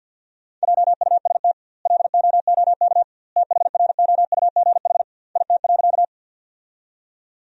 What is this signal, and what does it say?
Morse code, heard as a keyed tone, about 50 wpm, 700 Hz. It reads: JUST LOOK THROUGH IT.
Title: 2x Word-Spacing - Sentences from Top 200 Words - 50wpm